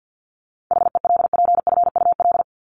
Morse code, heard as a keyed tone, about 48 wpm, 700 Hz. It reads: HELPFUL
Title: Top 751-1000 Words 50wpm